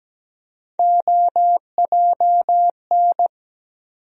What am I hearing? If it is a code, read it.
Morse code, heard as a keyed tone, about 17 wpm, 700 Hz. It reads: OJN
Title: USA Call Sign Postfixes - 3 Letters 17wpm